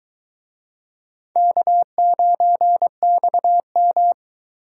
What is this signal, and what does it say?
Morse code, heard as a keyed tone, about 22 wpm, 700 Hz. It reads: K9XM